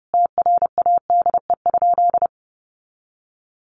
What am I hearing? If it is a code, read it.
Morse code, heard as a keyed tone, about 30 wpm, 700 Hz. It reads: TRADE?
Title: Sentences from Top 600 Words 30wpm